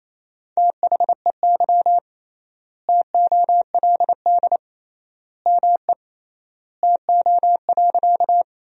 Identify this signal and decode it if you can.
Morse code, heard as a keyed tone, about 28 wpm, 700 Hz. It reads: THEY TOLD ME TO.
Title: 3x Word-Spacing - Sentences from Top 400 Words - 28wpm